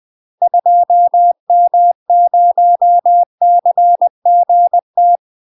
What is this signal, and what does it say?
Morse code, heard as a keyed tone, about 20 wpm, 700 Hz. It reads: 2M0CGT
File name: UK MASTER QSOs TRAINING / 20WPM / UK MASTER QSOs 0046